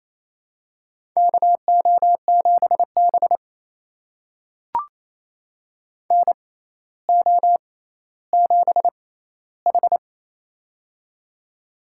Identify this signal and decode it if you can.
Morse code, heard as a keyed tone, about 28 wpm, spaced as for 14 wpm, 700 Hz. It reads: KO7B NO7H
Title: USA Call Signs - 6x Character Spacing - 28wpm